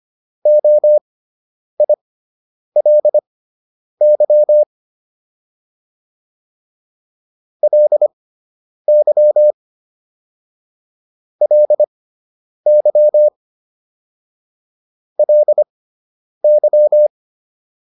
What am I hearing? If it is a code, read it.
Morse code, heard as a keyed tone, about 25 wpm, spaced as for 9 wpm, 600 Hz. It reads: OILY LY LY LY